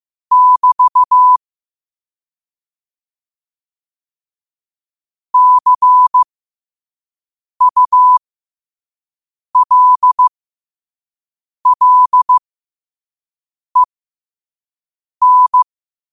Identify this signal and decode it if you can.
Morse code, heard as a keyed tone, about 15 wpm, spaced as for 5 wpm, 1000 Hz. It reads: <BT> CULLEN
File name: qotdfiles / 5wpm